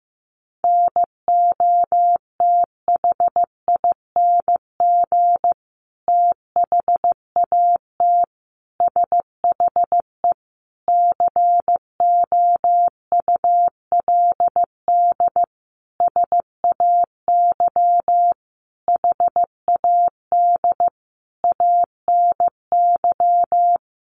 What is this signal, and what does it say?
Morse code, heard as a keyed tone, about 15 wpm, 700 Hz. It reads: NOTHING THAT SHE COULD SAY HAD ANY
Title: Sentences from Top 1100 Words 15wpm